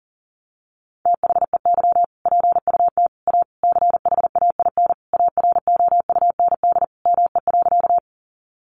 Morse code, heard as a keyed tone, about 40 wpm, 700 Hz. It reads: THEY PUT A CHAIN AROUND ME.
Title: Sentences from Top 1500 Words 40wpm